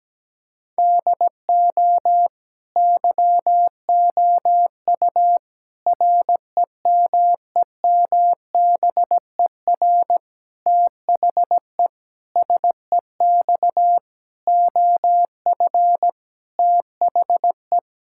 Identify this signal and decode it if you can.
Morse code, heard as a keyed tone, about 17 wpm, 700 Hz. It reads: DO YOU REMEMBER THE SEX OF THE
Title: Sentences from Top 1600 Words 17wpm